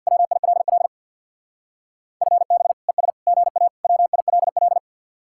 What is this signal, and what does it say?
Morse code, heard as a keyed tone, about 64 wpm, 700 Hz. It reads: WILL FB ES GA WILL